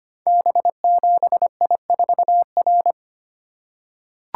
Morse code, heard as a keyed tone, about 25 wpm, 700 Hz. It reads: B7I4R
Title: Code Groups - 5 Letter:Numbers - 25wpm